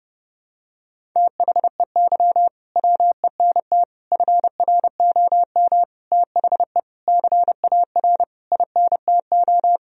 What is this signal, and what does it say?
Morse code, heard as a keyed tone, about 30 wpm, 700 Hz. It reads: THEY WENT FROM THE CAR INTO
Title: Sentences from Top 1000 Words - Review Part 1 - 30wpm